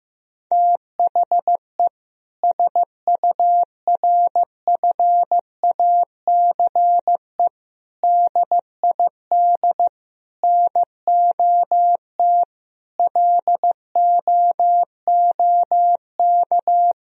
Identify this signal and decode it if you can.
Morse code, heard as a keyed tone, about 15 wpm, 700 Hz. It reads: THE SURFACE DID NOT LOOK
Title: Sentences from Top 1000 Words - Review Part 3 - 15wpm